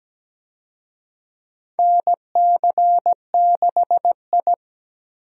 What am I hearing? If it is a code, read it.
Morse code, heard as a keyed tone, about 17 wpm, 700 Hz. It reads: NC6I
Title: USA Call Signs - 4 Characters - Repeated 2x - 17wpm